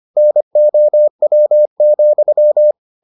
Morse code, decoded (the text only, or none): NOW,